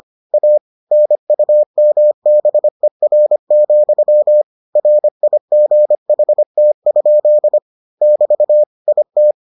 Morse code, heard as a keyed tone, about 25 wpm, 600 Hz. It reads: A NUMBER, RIGHT? <BT> IT